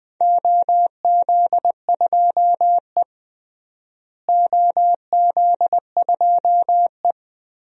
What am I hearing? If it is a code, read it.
Morse code, heard as a keyed tone, about 20 wpm, 700 Hz. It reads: OZ2E OZ2E
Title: DXCC-100 Call Signs - 4 Characters - Repeated 2x - 20wpm